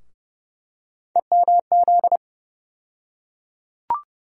Morse code, heard as a keyed tone, about 30 wpm, 700 Hz. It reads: EMZ
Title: USA Call Sign Postfixes - 3 Letters 30wpm